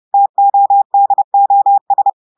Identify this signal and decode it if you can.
Morse code, heard as a keyed tone, about 30 wpm, 800 Hz. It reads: TODOS
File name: audios / portugais / portugais texte en clair 2 vitesse 30 mots minutes